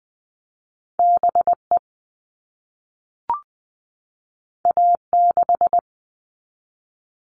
Morse code, heard as a keyed tone, about 20 wpm, 700 Hz. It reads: B E A 6
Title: All Cut Numbers - 2-Digit 20wpm